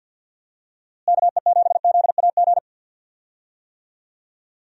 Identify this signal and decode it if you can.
Morse code, heard as a keyed tone, about 50 wpm, 700 Hz. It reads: KE7ZAG
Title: USA Call Signs - 6 Characters - Repeated 3x - 50wpm